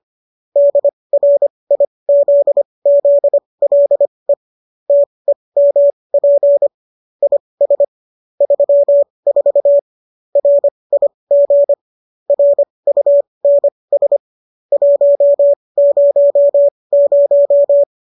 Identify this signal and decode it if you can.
Morse code, heard as a keyed tone, about 25 wpm, spaced as for 20 wpm, 570 Hz. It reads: DRIZZLE TEMP IS 34 RIG RUNS 100